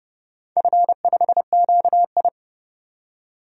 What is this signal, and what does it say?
Morse code, heard as a keyed tone, about 32 wpm, 700 Hz. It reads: F5QI
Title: DXCC-100 Call Signs - 4 Characters - 30wpm